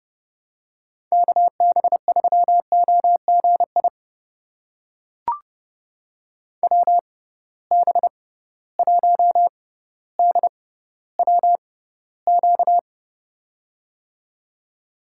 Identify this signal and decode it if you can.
Morse code, heard as a keyed tone, about 30 wpm, spaced as for 15 wpm, 700 Hz. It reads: KB3OGI WB1DWQ
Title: USA Call Signs - 6 Characters - 6x Character Spacing - 30wpm